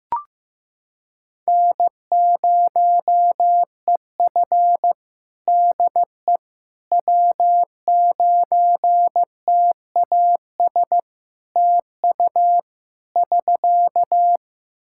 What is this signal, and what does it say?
Morse code, heard as a keyed tone, about 16 wpm, 700 Hz. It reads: N0EF DE W9TAS TU <SK>
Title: Practice QSO - Exchange 4 - Ending - 15wpm